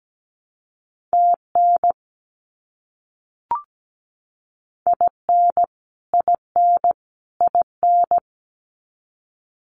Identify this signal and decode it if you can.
Morse code, heard as a keyed tone, about 17 wpm, 700 Hz. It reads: TN IN IN IN
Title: US States Repeated 3x 17wpm